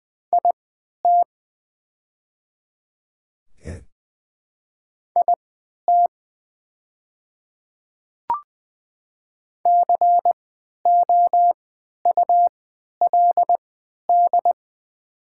Morse code, heard as a keyed tone, about 20 wpm, spaced as for 11 wpm, 690 Hz. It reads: IT IT COULD
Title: 3x Character Spacing - Top 200 Words - 20wpm